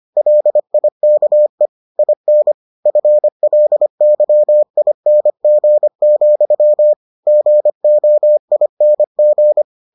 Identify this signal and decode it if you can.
Morse code, heard as a keyed tone, about 25 wpm, 600 Hz. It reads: LIKE IN FLYING, GOING